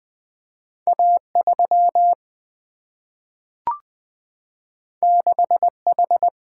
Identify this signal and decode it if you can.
Morse code, heard as a keyed tone, about 20 wpm, 700 Hz. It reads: A 3 6 H
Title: Single-Letter Single-Number 20wpm